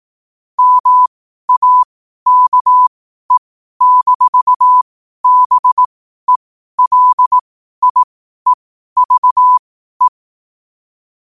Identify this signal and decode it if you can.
Morse code, heard as a keyed tone, about 18 wpm, spaced as for 12 wpm, 1000 Hz. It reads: MAKE-BELIEVE
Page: Quotes for Fri, 15 Aug 2025 in Morse Code at 12 words per minute.